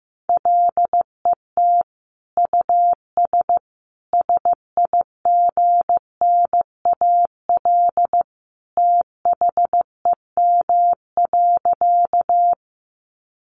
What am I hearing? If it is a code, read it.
Morse code, heard as a keyed tone, about 15 wpm, 700 Hz. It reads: LET US SIGNAL THEM.
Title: Sentences from Top 1400 Words 15wpm